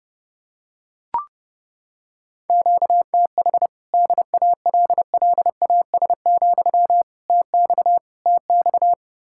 Morse code, heard as a keyed tone, about 31 wpm, 700 Hz. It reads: QTH DALLAS, TX TX